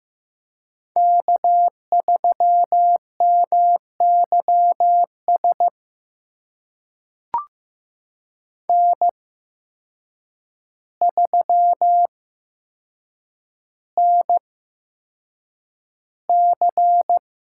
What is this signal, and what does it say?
Morse code, heard as a keyed tone, about 15 wpm, spaced as for 7 wpm, 700 Hz. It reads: K3MYS N3NC
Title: USA Call Signs - 5 Characters - 8x Character Spacing - 15wpm